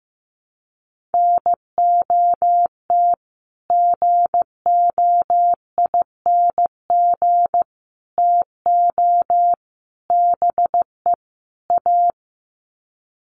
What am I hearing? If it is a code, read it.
Morse code, heard as a keyed tone, about 15 wpm, 700 Hz. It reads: NOT GOING TO BE A
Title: Sets of 5 Words Encore 15wpm